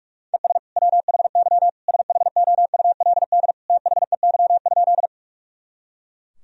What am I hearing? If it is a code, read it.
Morse code, heard as a keyed tone, about 46 wpm, 700 Hz. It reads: ES WHY SHOULD THEY?
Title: Sentences from Top 300 Words and Abbreviations 45wpm